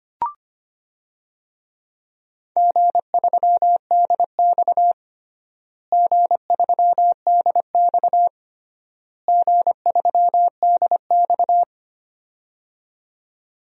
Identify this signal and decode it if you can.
Morse code, heard as a keyed tone, about 25 wpm, 700 Hz. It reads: G3DX G3DX G3DX
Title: DXCC-100 Call Signs - 4 Characters - Repeated 3x - 25wpm